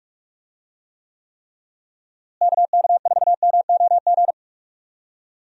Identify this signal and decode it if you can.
Morse code, heard as a keyed tone, about 42 wpm, 700 Hz. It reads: KK4MOG
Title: USA Call Signs - 6 Characters - Repeated 3x - 45wpm